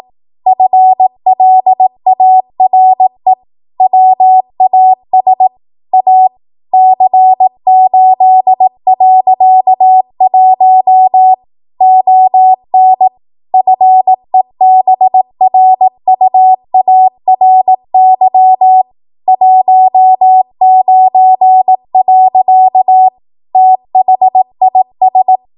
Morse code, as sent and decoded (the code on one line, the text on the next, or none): ..-. .-.. .- .-. / .-- .- ... / .- / -.-. ---.. .-.-.- .---- / --- -. / ..-. -... .-. ..- .- .-. -.-- / .---- ----. .-.-.- / - .... .. ...
FLARE WAS A C8.1 ON FEBRUARY 19. THIS